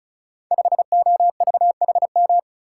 Morse code, heard as a keyed tone, about 36 wpm, 700 Hz. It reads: 5OVHM
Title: Code Groups - 5 Letter:Numbers - 35wpm